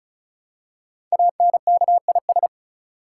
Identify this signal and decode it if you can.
Morse code, heard as a keyed tone, about 36 wpm, 700 Hz. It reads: ANKIS
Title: Code Groups - 5 Letters - 35wpm